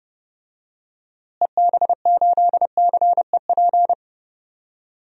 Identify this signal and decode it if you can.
Morse code, heard as a keyed tone, about 30 wpm, 700 Hz. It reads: EB8CEP
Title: DXCC-100 Call Signs - 6 Characters - Repeated 3x - 30wpm